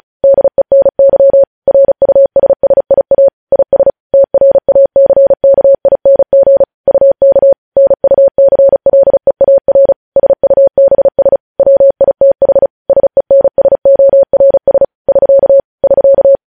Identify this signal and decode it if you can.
Morse code, heard as a keyed tone, about 36 wpm, 550 Hz. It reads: DENY RUSSIA IS TRACKING UK NUCLEAR SUBS WITH SENSORS <SK> <SK>
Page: News Headlines in Morse Code at 35 WPM